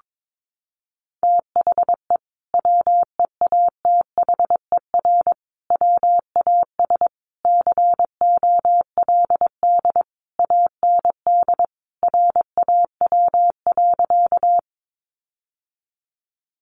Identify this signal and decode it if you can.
Morse code, heard as a keyed tone, about 22 wpm, 700 Hz. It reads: THE WEATHER WAS COLD AND RAW.
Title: Sentences from Top 1500 Words 22wpm